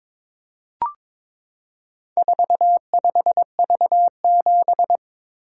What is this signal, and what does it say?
Morse code, heard as a keyed tone, about 23 wpm, 700 Hz. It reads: 45V7
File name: practice / 22wpm / Four Letter-Number Combinations - 22wpm